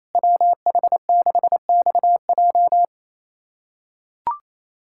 Morse code, heard as a keyed tone, about 28 wpm, 700 Hz. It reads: WH6XJ
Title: USA Callsign DE Callsign - 5 Characters - 28wpm